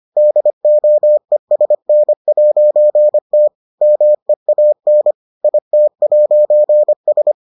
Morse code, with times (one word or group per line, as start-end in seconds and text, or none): 0.2-3.5 DOESN'T
3.8-5.1 MEAN
5.4-7.3 IT'S